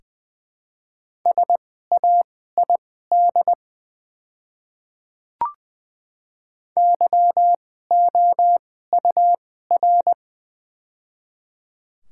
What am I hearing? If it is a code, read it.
Morse code, heard as a keyed tone, about 20 wpm, spaced as for 14 wpm, 700 Hz. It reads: SAID YOUR
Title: 2x Character Spacing - Top 100 Words - 20wpm